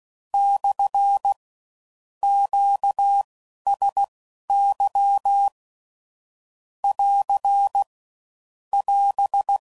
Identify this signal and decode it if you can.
Morse code, heard as a keyed tone, about 16 wpm, spaced as for 12 wpm, 800 Hz. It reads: / QSY <AR> <AS>
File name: CW Code Groups @15WPM / Q Calls / Q Calls SPSL